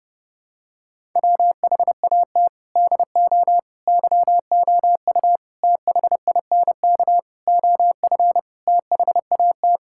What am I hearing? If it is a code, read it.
Morse code, heard as a keyed tone, about 30 wpm, 700 Hz. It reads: WHAT DO YOU THINK OF THAT
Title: Sentences from Top 1000 Words - Review Part 6 - 30wpm